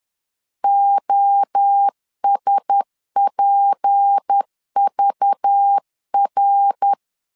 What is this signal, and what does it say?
Morse code, heard as a keyed tone, about 10 wpm, 800 Hz. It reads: OSPVR
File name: LEZIONI / 8 YJ45 / 8 YJ45 10wpm 2549